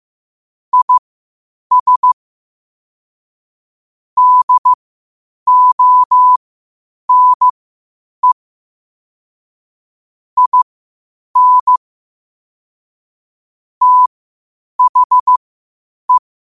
Morse code, heard as a keyed tone, about 15 wpm, spaced as for 8 wpm, 1000 Hz. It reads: IS DONE IN THE
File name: qotdfiles / 8wpm